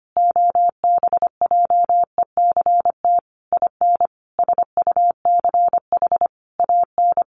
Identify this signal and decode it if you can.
Morse code, heard as a keyed tone, about 25 wpm, 700 Hz. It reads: OBJECT IN SUCH AN
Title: Sentences from Top 1500 Words 25wpm